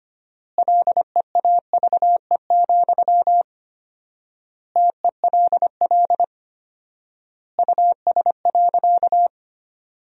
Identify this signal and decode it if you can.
Morse code, heard as a keyed tone, about 25 wpm, 700 Hz. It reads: LEAVE, TELL US.
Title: 4x Word-Spacing - Sentences from Top 400 Words - 25wpm